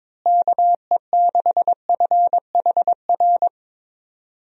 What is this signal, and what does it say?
Morse code, heard as a keyed tone, about 22 wpm, 700 Hz. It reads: KE6FHR
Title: USA Call Signs - 6 Characters - Repeated 2x - 22wpm